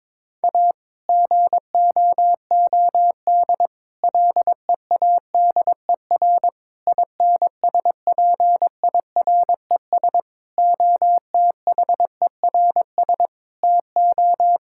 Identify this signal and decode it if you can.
Morse code, heard as a keyed tone, about 22 wpm, 700 Hz. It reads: A GOOD LEADER INSPIRES OTHERS TO
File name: Difficult Sentences with Hints - 22wpm